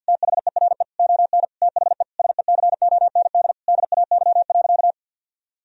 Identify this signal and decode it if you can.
Morse code, heard as a keyed tone, about 48 wpm, 700 Hz. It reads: THERE ON THE SECOND DAY.
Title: Sentences from Top 1000 Words - Review Part 4 - 50wpm